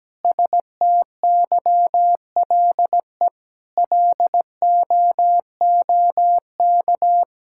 Morse code, heard as a keyed tone, about 17 wpm, 700 Hz. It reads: STYLE LOOK
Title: Sentences from Top 1000 Words - Review Part 5 - 17wpm